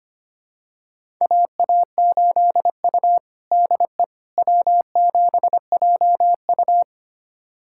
Morse code, heard as a keyed tone, about 25 wpm, 700 Hz. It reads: AA8U DE W7JU
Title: USA Callsign DE Callsign 25wpm